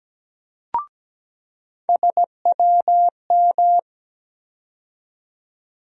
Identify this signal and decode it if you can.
Morse code, heard as a keyed tone, about 17 wpm, 700 Hz. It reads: SWM
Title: USA Call Sign Postfixes - 3 Letters 17wpm